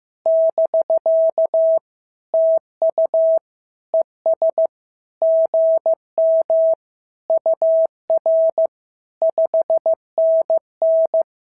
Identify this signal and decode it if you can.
Morse code, heard as a keyed tone, about 15 wpm, 650 Hz. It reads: <BK> TU ES GM UR 5NN
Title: SOTA QSO Practice 15wpm